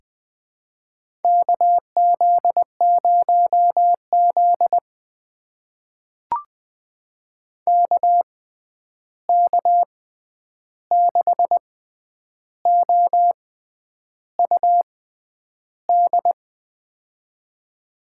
Morse code, heard as a keyed tone, about 20 wpm, spaced as for 9 wpm, 700 Hz. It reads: KZ0Z KK6OUD
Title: USA Call Signs - 6x Character Spacing - 20wpm